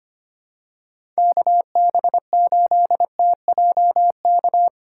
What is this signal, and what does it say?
Morse code, heard as a keyed tone, about 25 wpm, 700 Hz. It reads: KB8TJK
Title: USA Call Signs - 6 Characters - 3x Character Spacing - 25wpm